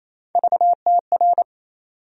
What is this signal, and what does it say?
Morse code, heard as a keyed tone, about 28 wpm, 700 Hz. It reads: VTR
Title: USA Call Sign Postfixes - 3 Letters 28wpm